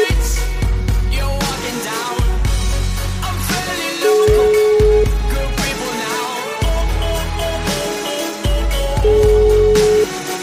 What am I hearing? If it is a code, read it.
Ringback tone (Europe, 425 Hz), with noise.